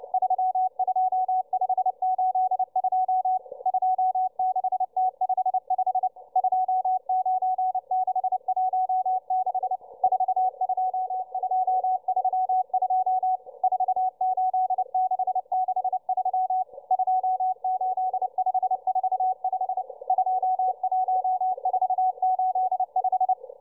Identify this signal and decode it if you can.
Morse code, heard as a keyed tone, about 29 wpm, 750 Hz, with radio noise. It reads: 32582 26T55 29616 42232 48663 28545 21485